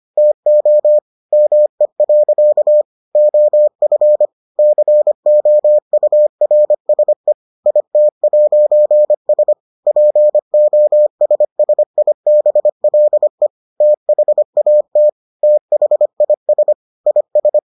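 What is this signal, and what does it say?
Morse code, heard as a keyed tone, about 25 wpm, 600 Hz. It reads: TO ME. OF COURSE IT'S POSSIBLE THAT THIS IS
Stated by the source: Morse edition of the Foundations of Amateur Radio episodes.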